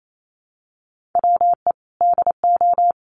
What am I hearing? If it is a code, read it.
Morse code, heard as a keyed tone, about 28 wpm, 700 Hz. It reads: WE DO
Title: Sets of 2 Words 28wpm